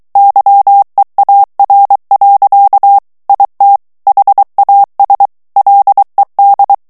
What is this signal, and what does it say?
Morse code, heard as a keyed tone, about 23 wpm, 800 Hz. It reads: YEAR. IT HAS LED
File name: morse mp3 / ANGLAIS texte clair 2 vitesse 22 mots minute